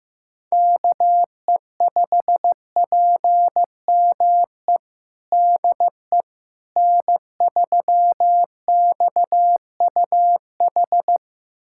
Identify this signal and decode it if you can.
Morse code, heard as a keyed tone, about 16 wpm, 700 Hz. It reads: KE5PME DE N3XUH
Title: USA Callsign DE Callsign 15wpm